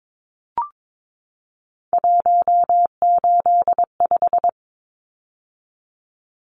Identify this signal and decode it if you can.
Morse code, heard as a keyed tone, about 22 wpm, 700 Hz. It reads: 185
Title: Numbers - 3-Digit 22wpm